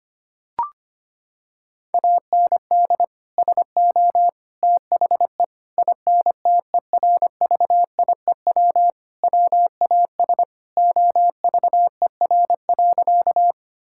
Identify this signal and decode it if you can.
Morse code, heard as a keyed tone, about 25 wpm, 700 Hz. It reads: AND SO THE INTERVIEW WAS OVER.
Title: Sentences from Top 1600 Words 25wpm